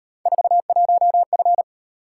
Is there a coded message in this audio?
Yes: Morse code from 0.3 to 1.6 s, heard as a keyed tone, about 38 wpm, 700 Hz. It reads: 41F